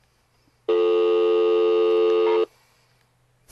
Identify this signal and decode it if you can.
Dial tone (North America), 350 + 440 Hz.